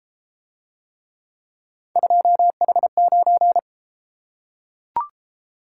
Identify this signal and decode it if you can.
Morse code, heard as a keyed tone, about 29 wpm, 700 Hz. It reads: 2H9